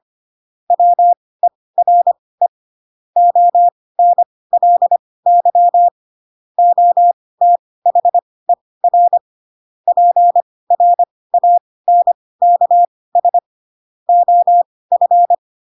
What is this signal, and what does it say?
Morse code, heard as a keyed tone, about 25 wpm, spaced as for 18 wpm, 700 Hz. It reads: WERE ONLY OTHER PRANKS OF
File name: Pinocchio / Pinocchio ch7 12 25 ik1hge